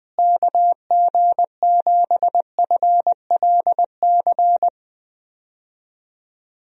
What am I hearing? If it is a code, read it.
Morse code, heard as a keyed tone, about 20 wpm, 700 Hz. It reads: KG7FLC
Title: USA Call Signs - 6 Characters - Repeated 2x - 20wpm